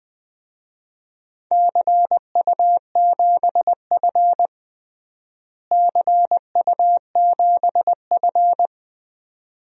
Morse code, heard as a keyed tone, about 20 wpm, 700 Hz. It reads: CU7F CU7F
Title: DXCC-100 Call Signs - 4 Characters - Repeated 2x - 20wpm